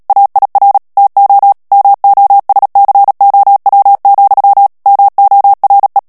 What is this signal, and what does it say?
Morse code, heard as a keyed tone, about 37 wpm, 800 Hz. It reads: AIR TO MOSCOW, MORE
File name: morse mp3 / ANGLAIS texte clair 2 vitesse 35 mots minute